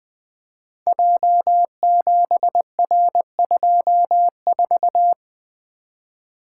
Morse code, heard as a keyed tone, about 20 wpm, 700 Hz. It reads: J7R24
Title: Code Groups - 5 Letter:Numbers - 20wpm